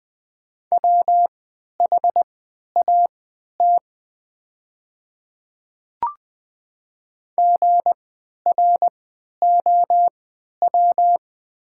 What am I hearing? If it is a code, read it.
Morse code, heard as a keyed tone, about 20 wpm, spaced as for 11 wpm, 700 Hz. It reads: WHAT GROW
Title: 3x Character Spacing - Top 500 Words - 20wpm